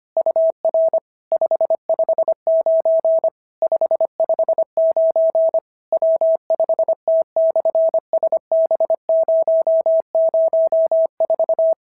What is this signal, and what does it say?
Morse code, heard as a keyed tone, about 25 wpm, 650 Hz. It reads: UR 559 559 W5T/SB004
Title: SOTA QSO Practice 25wpm